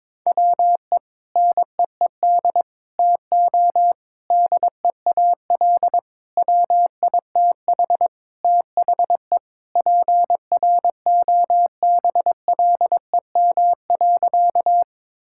Morse code, heard as a keyed tone, about 22 wpm, 700 Hz. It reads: WE NEED TO DEAL WITH THE PROBLEM.
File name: practice / 22wpm / Sentences with 4-Word Phrases - 22wpm